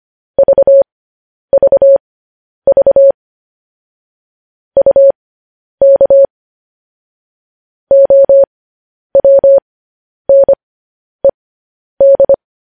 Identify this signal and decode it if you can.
Morse code, heard as a keyed tone, about 25 wpm, spaced as for 10 wpm, 550 Hz. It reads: VVV UK OWNED